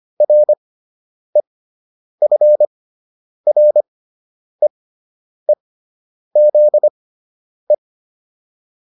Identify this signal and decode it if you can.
Morse code, heard as a keyed tone, about 25 wpm, spaced as for 9 wpm, 600 Hz. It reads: REFREEZE